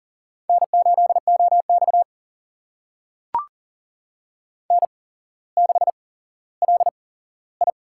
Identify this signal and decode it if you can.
Morse code, heard as a keyed tone, about 38 wpm, spaced as for 17 wpm, 700 Hz. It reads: N8OX N6LI
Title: USA Call Signs - 4 Characters - 8x Character Spacing - 40wpm